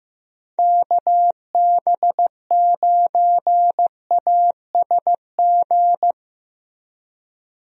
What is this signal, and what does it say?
Morse code, heard as a keyed tone, about 15 wpm, 700 Hz. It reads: KB9ASG
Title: USA Call Signs - 6 Characters - Repeated 3x - 15wpm